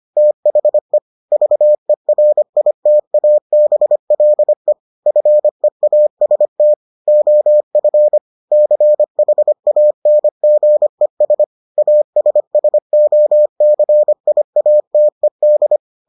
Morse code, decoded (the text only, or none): THE VERITABLE FEAST OF CHANGES ASSOCIATED